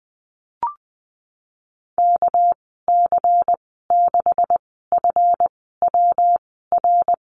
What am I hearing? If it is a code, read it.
Morse code, heard as a keyed tone, about 20 wpm, spaced as for 15 wpm, 700 Hz. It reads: KC6FWR
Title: 207 Lesson 33g - Letter j - Call Signs 20wpm